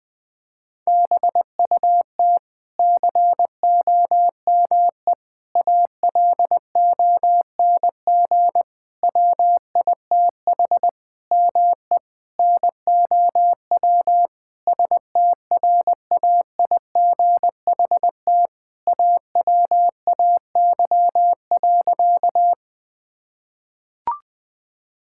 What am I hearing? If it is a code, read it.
Morse code, heard as a keyed tone, about 21 wpm, 700 Hz. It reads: BUT COME ALONG WITH ME NOW STRAIGHT AWAY.
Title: Sentences from Top 1000 Words - Review Part 6 - 20wpm